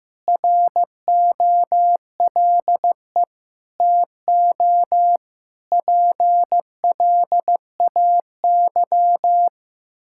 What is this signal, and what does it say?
Morse code, heard as a keyed tone, about 15 wpm, 700 Hz. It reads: ROLE TO PLAY
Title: Sentences from Top 1000 Words - Review Part 6 - 15wpm